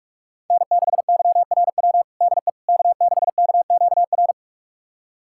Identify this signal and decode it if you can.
Morse code, heard as a keyed tone, about 42 wpm, 700 Hz. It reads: N6YRW DE K6KQR
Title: USA Callsign DE Callsign - 5 Characters - 45wpm